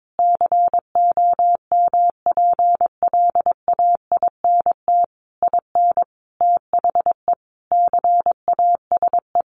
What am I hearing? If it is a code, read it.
Morse code, heard as a keyed tone, about 22 wpm, 700 Hz. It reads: COMPLAINT IN THE CASE
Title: Sentences from Top 1400 Words 22wpm